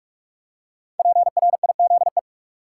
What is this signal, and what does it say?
Morse code, heard as a keyed tone, about 46 wpm, 700 Hz. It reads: PRIZE
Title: Five Letter Words - 45wpm